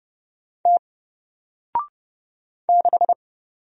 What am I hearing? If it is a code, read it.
Morse code, heard as a keyed tone, about 30 wpm, 700 Hz. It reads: T 6
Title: Single Character (Letter:Number:Punctuation:Prosign) Alt 30wpm